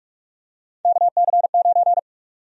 Morse code, heard as a keyed tone, about 41 wpm, 700 Hz. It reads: KC9
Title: USA Call Sign Prefixes - 45wpm